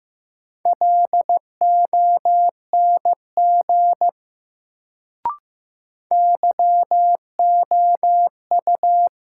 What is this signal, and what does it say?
Morse code, heard as a keyed tone, about 15 wpm, 700 Hz. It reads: LONG YOU
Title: Sentences from Top 100 Words - New Word Drill - 15wpm